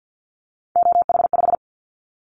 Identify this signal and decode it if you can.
Morse code, heard as a keyed tone, about 50 wpm, 700 Hz. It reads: OH5
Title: Callsign Prefixes for Top 50 DXCC - 50wpm